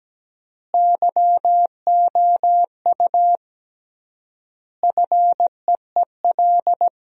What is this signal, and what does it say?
Morse code, heard as a keyed tone, about 17 wpm, 700 Hz. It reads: YOU FEEL
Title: 3x Word-Spacing - Sentences from Top 400 Words - 17wpm